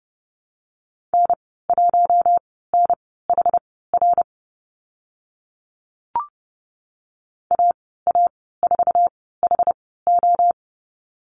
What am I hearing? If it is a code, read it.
Morse code, heard as a keyed tone, about 30 wpm, spaced as for 17 wpm, 700 Hz. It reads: N1NHR AA4HO
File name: Lesson 14g - Number 2 - Call Signs 30wpm